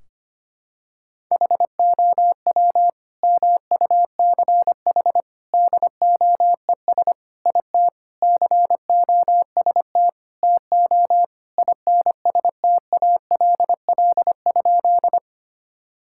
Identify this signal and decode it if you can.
Morse code, heard as a keyed tone, about 25 wpm, 700 Hz. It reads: HOW MUCH DOES IT COST TO INSTALL?
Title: Sentences from Top 1600 Words 25wpm